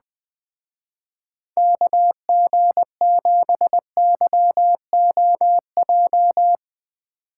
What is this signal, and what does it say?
Morse code, heard as a keyed tone, about 20 wpm, 700 Hz. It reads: KG7YOJ
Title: Speed Racing - USA 2x3 Call Signs 20wpm